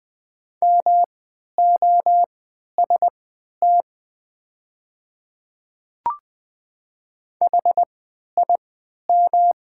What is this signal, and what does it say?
Morse code, heard as a keyed tone, about 20 wpm, spaced as for 11 wpm, 700 Hz. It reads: MOST HIM
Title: 3x Character Spacing - Top 100 Words - 20wpm